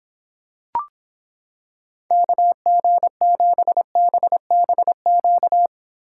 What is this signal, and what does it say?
Morse code, heard as a keyed tone, about 26 wpm, 700 Hz. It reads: KG7BBQ